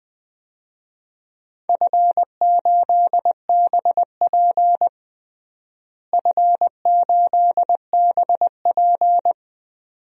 Morse code, heard as a keyed tone, about 20 wpm, 700 Hz. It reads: F8BP F8BP
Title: DXCC-100 Call Signs - 4 Characters - Repeated 2x - 20wpm